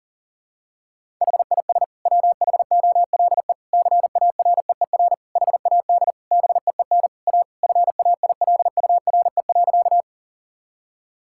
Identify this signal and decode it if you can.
Morse code, heard as a keyed tone, about 40 wpm, 700 Hz. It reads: HIS WHOLE CAREER HAD BEEN A FAILURE.
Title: Sentences from Top 1000 Words - Review Part 6 - 40wpm